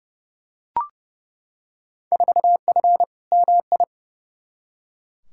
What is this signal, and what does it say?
Morse code, heard as a keyed tone, about 31 wpm, 700 Hz. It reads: 4F MI
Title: Field Day Exchanges - 30wpm